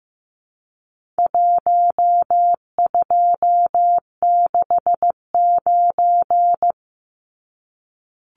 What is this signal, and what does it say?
Morse code, heard as a keyed tone, about 15 wpm, 700 Hz. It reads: 1269
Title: Numbers - Contest Style - 4-Digit 15wpm